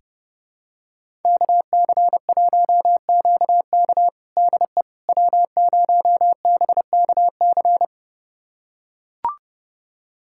Morse code, heard as a keyed tone, about 30 wpm, 700 Hz. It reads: KC1QK DE W0BKC
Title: USA Callsign DE Callsign - 5 Characters - 30wpm